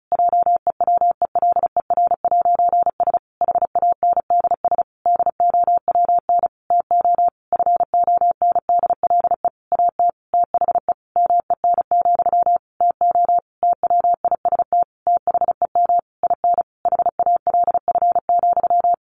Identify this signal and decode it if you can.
Morse code, heard as a keyed tone, about 34 wpm, 700 Hz. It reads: JEWELER'S HANDS DOWN TO FONDLE AT THE MEN, TO TWIST THEM IN HALF,
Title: A Sound Of Thunder 35wpm